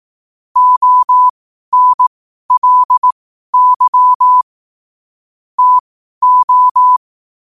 Morse code, heard as a keyed tone, about 18 wpm, spaced as for 12 wpm, 1000 Hz. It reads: ONLY TO
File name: qotdfiles / 12wpm